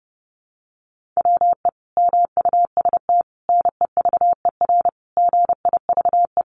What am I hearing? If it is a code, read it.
Morse code, heard as a keyed tone, about 30 wpm, 700 Hz. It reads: WE MUST NEVER GIVE
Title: Sentences from Top 1500 Words 30wpm